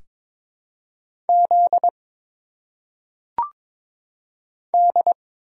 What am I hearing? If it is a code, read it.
Morse code, heard as a keyed tone, about 22 wpm, 700 Hz. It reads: Z D
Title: Single Letters - Phonetic Pronunciation 22wpm